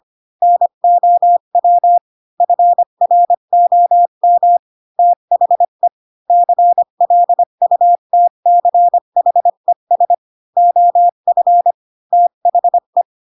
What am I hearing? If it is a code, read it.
Morse code, heard as a keyed tone, about 25 wpm, 700 Hz. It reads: NOW FROM THE CLUTCHES OF THE
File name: Pinocchio / Pinocchio ch4 20 25 ik1hge